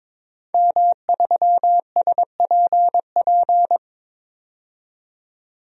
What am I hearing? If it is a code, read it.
Morse code, heard as a keyed tone, about 22 wpm, 700 Hz. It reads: M3SPP
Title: DXCC-100 Call Signs - 5 Characters - 22wpm